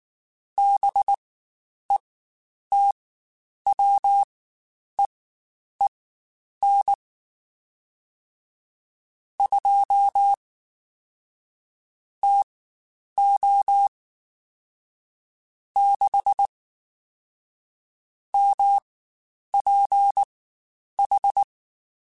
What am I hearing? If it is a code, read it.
Morse code, heard as a keyed tone, about 19 wpm, spaced as for 9 wpm, 800 Hz. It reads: BETWEEN 2 TO 6 MPH
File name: Spaced Characters / Spaced Characters-mp3s / Weather 4 10 sp @18